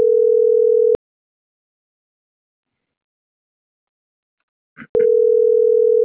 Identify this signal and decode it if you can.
Ringback tone (North America), 440 + 480 Hz.